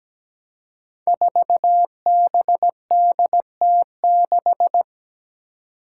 Morse code, heard as a keyed tone, about 17 wpm, 700 Hz. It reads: 4BDT6
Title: Five Challenging Letter-Number Combinations 17wpm